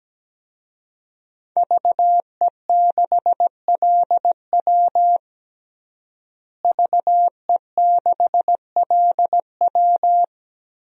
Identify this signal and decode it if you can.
Morse code, heard as a keyed tone, about 17 wpm, 700 Hz. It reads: VE6LW VE6LW
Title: DXCC-100 Call Signs - 5 Characters - Repeated 3x - 17wpm